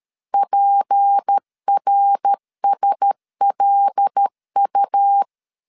Morse code, heard as a keyed tone, about 13 wpm, 800 Hz. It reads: PRSLU
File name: CW / DET / 1 6 1500C S A 12wpm 3440